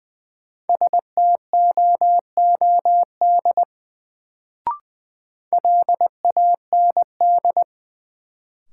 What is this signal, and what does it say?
Morse code, heard as a keyed tone, about 20 wpm, 700 Hz. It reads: STOOD LAND
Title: Sentences from Top 500 Words - New Word Drill - 20wpm